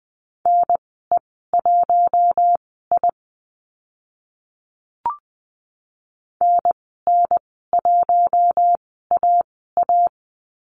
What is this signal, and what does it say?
Morse code, heard as a keyed tone, about 20 wpm, spaced as for 14 wpm, 700 Hz. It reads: NE1I NN1AA
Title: 042 Lesson 8g - Number 1 - Call Signs 20wpm